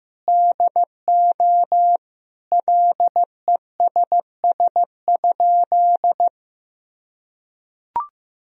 Morse code, heard as a keyed tone, about 16 wpm, 700 Hz. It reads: DO LESS?
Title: Sentences from Top 1000 Words - Review Part 2 - 15wpm